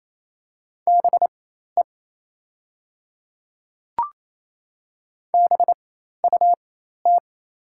Morse code, heard as a keyed tone, about 27 wpm, spaced as for 13 wpm, 690 Hz. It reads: BE BUT
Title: 4x Character Spacing - Top 200 Words - 28wpm